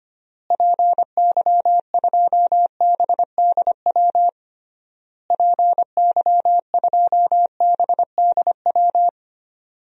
Morse code, heard as a keyed tone, about 25 wpm, 700 Hz. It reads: PY2BDW PY2BDW
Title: DXCC-100 Call Signs - 6 Characters - Repeated 3x - 25wpm